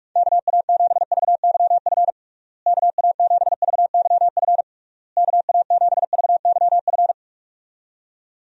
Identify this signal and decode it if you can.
Morse code, heard as a keyed tone, about 46 wpm, 700 Hz. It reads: KA7VYF KA7VYF KA7VYF
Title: USA Call Signs - 6 Characters - Repeated 3x - 45wpm